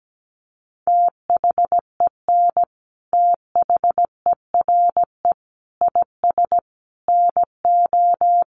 Morse code, heard as a keyed tone, about 17 wpm, 700 Hz. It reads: THEN THERE IS NO
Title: Sentences from Top 700 Words 17wpm